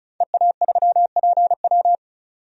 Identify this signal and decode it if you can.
Morse code, heard as a keyed tone, about 36 wpm, 700 Hz. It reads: EA3PW
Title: DXCC-100 Call Signs - 5 Characters - 35wpm